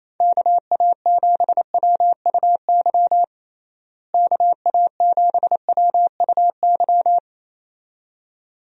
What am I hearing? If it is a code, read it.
Morse code, heard as a keyed tone, about 27 wpm, 700 Hz. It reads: KA7WUY KA7WUY
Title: USA Call Signs - 6 Characters - Repeated 2x - 28wpm